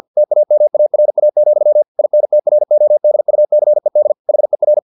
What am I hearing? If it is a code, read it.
Morse code, heard as a keyed tone, about 50 wpm, 600 Hz. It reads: TAMARA, INTRODUCED HER
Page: Never 49wpm